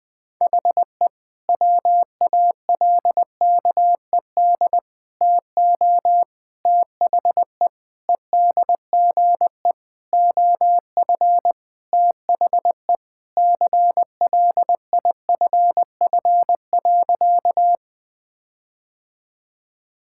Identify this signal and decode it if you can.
Morse code, heard as a keyed tone, about 21 wpm, 700 Hz. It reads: HE WALKED TO THE EDGE OF THE CLIFF.
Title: Sentences with 5-Word Phrases - 20wpm